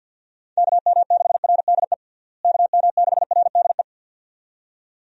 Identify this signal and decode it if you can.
Morse code, heard as a keyed tone, about 51 wpm, 700 Hz. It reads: KM6RDE KM6RDE
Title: USA Call Signs - 6 Characters - Repeated 2x - 50wpm